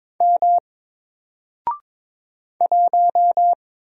Morse code, heard as a keyed tone, about 22 wpm, 700 Hz. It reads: M 1
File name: Single-Letter-Number-22wpm